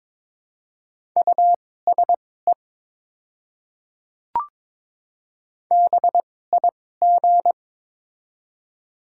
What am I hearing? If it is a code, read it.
Morse code, heard as a keyed tone, about 22 wpm, spaced as for 16 wpm, 700 Hz. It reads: USE BIG
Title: 2x Character Spacing - Top 200 Words - 22wpm